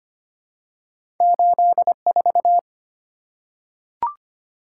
Morse code, heard as a keyed tone, about 25 wpm, 690 Hz. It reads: 84
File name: Two Letter-Number Combinations - 25wpm